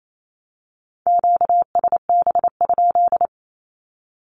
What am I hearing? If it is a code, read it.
Morse code, heard as a keyed tone, about 28 wpm, 700 Hz. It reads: QSB?
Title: Q-Codes 28wpm